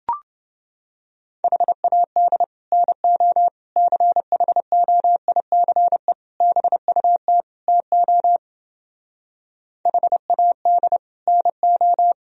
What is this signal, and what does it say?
Morse code, heard as a keyed tone, about 29 wpm, 700 Hz. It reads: HAD NO CHOICE BUT TO HAD NO